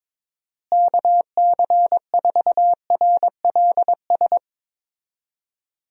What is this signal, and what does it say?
Morse code, heard as a keyed tone, about 21 wpm, 700 Hz. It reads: KC4RLS
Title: USA Call Signs - 6 Characters - Repeated 2x - 22wpm